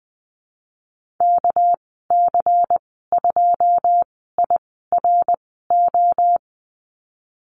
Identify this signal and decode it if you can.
Morse code, heard as a keyed tone, about 19 wpm, spaced as for 14 wpm, 700 Hz. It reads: KC2IRO
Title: 213 Lesson 34g - Number 8 - Call Signs 20wpm